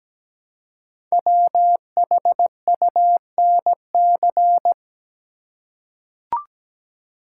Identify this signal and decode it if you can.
Morse code, heard as a keyed tone, about 17 wpm, 700 Hz. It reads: WHUNC
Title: Code Groups - 5 Letters - 17wpm